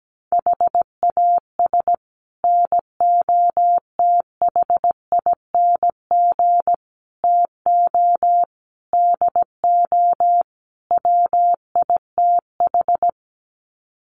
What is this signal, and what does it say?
Morse code, heard as a keyed tone, about 17 wpm, 700 Hz. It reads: HAS NOTHING TO DO WITH
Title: Sets of 5 Words 17wpm